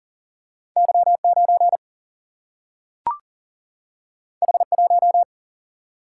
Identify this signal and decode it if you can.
Morse code, heard as a keyed tone, about 40 wpm, 700 Hz. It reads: Y 9 H 1
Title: Single-Letter Single-Number 40wpm